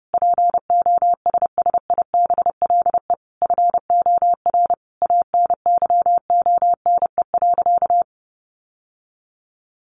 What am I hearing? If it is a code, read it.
Morse code, heard as a keyed tone, about 30 wpm, 700 Hz. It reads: POSSIBLE FOR ANYONE.
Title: Sentences from Top 900 Words 30wpm